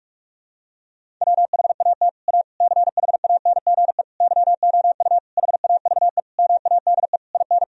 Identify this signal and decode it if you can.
Morse code, heard as a keyed tone, about 41 wpm, 700 Hz. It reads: WHAT A CHANGE YOU HAVE MADE IN
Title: Sentences from Top 1000 Words - Review Part 1 - 45wpm